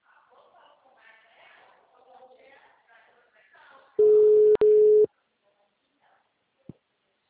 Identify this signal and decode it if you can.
Ringback tone (Europe, 425 Hz), with speech.
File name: out- -2024- - - .194